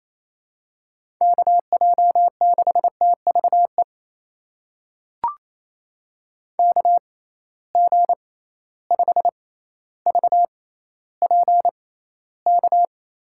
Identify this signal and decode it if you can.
Morse code, heard as a keyed tone, about 28 wpm, spaced as for 14 wpm, 700 Hz. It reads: KJ6TVE KG5VPK
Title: USA Call Signs - 6 Characters - 6x Character Spacing - 28wpm